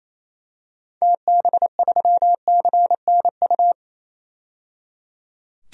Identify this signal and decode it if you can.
Morse code, heard as a keyed tone, about 28 wpm, 700 Hz. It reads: TB3CNU
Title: DXCC-100 Call Signs - 6 Characters - 28wpm